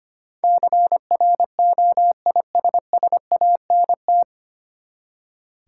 Morse code, heard as a keyed tone, about 25 wpm, 690 Hz. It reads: CROISSANT
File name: practice / 25wpm / Easy Words with Hints - 25wpm